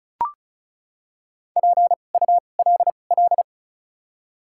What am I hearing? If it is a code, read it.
Morse code, heard as a keyed tone, about 36 wpm, 700 Hz. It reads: P U L L
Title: 2x Character Spacing - Top 500 Words - 35wpm